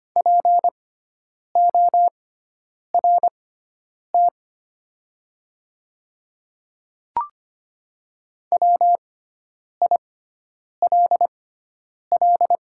Morse code, heard as a keyed tone, about 25 wpm, spaced as for 9 wpm, 700 Hz. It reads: PORT WILL
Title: 6x Character Spacing - Top 300 Words - 25wpm